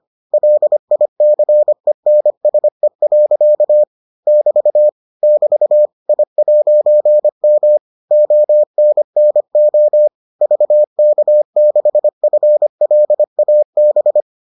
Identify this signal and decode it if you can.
Morse code, heard as a keyed tone, about 25 wpm, 600 Hz. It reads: LICENSE. <BT> <BT> I'M ONNO VK6FLAB